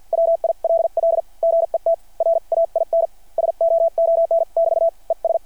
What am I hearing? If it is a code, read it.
Morse code, heard as a keyed tone, about 48 wpm, spaced as for 42 wpm, 660 Hz. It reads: WILL GET RAIN SOON<BT> ES
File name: archived / CRSnet